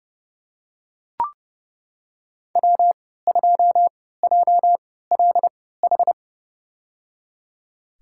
Morse code, heard as a keyed tone, about 30 wpm, spaced as for 17 wpm, 700 Hz. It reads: W2JLH
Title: USA Call Signs - 5 Characters - 3x Character Spacing - 30wpm